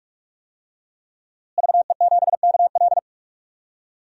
Morse code, heard as a keyed tone, about 46 wpm, 700 Hz. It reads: VE7KL
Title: DXCC-100 Call Signs - 5 Characters - Repeated 3x - 45wpm